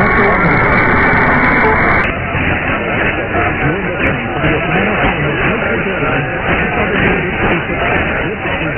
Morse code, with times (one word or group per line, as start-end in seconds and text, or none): none